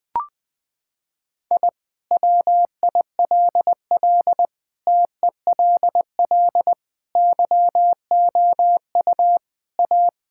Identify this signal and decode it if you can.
Morse code, heard as a keyed tone, about 21 wpm, 700 Hz. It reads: I WILL TELL YOU A
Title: Sentences with 2-Word Phrases - 20wpm